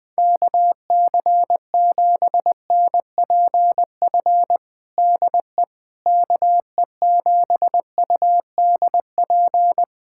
Morse code, heard as a keyed tone, about 21 wpm, 700 Hz. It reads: KC7NPF DE KE7UDP
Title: USA Callsign DE Callsign - 6 Characters - 20wpm